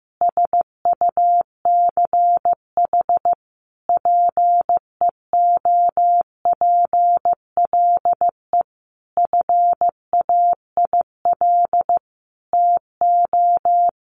Morse code, heard as a keyed tone, about 15 wpm, 700 Hz. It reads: SUCH PEOPLE FAIL TO
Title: Sentences from Top 1100 Words 15wpm